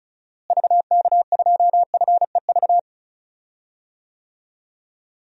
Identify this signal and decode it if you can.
Morse code, heard as a keyed tone, about 36 wpm, 700 Hz. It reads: VK2FEV
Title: DXCC-100 Call Signs - 6 Characters - Repeated 2x - 35wpm